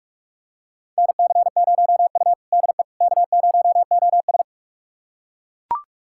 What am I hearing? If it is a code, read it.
Morse code, heard as a keyed tone, about 42 wpm, 700 Hz. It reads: NC0U DE K0OS
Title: USA Callsign DE Callsign - 4 Characters - 45wpm